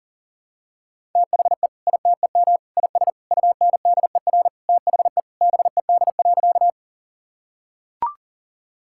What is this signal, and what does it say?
Morse code, heard as a keyed tone, about 40 wpm, 700 Hz. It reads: THE ITEM IS UNDER THE BED.
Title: Sentences from Top 1000 Words - Review Part 6 - 40wpm